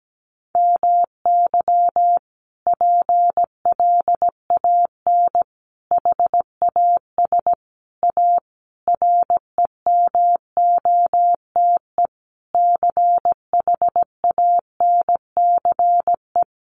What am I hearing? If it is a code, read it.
Morse code, heard as a keyed tone, about 17 wpm, 700 Hz. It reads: MY PLAN HAS A REMOTE CHANCE
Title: Sentences from Top 1300 Words 17wpm